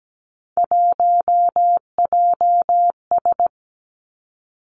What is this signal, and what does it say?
Morse code, heard as a keyed tone, about 17 wpm, 700 Hz. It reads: 1JS
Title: Three Challenging Letter-Number Combinations 17wpm